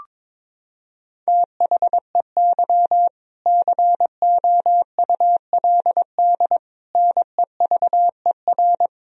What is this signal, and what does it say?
Morse code, heard as a keyed tone, about 21 wpm, 700 Hz. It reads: THEY COULD NEVER
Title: Sentences from Top 1000 Words - Review Part 4 - 22wpm